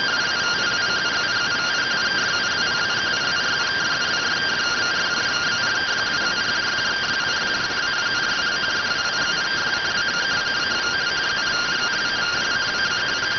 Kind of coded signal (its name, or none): two-tone teleprinter signal